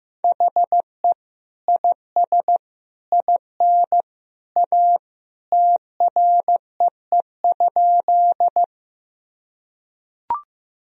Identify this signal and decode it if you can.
Morse code, heard as a keyed tone, about 16 wpm, 700 Hz. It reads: HE IS IN A TREE?
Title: Sentences from Top 1000 Words - Review Part 1 - 15wpm